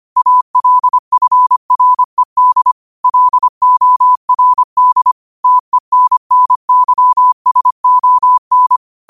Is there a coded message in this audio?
Yes: Morse code from 0.2 to 8.8 s, heard as a keyed tone, about 25 wpm, 1000 Hz. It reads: ALFRED LORD TENNYSON